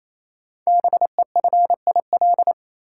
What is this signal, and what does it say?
Morse code, heard as a keyed tone, about 27 wpm, 700 Hz. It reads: BEFIL
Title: Code Groups - 5 Letters - 28wpm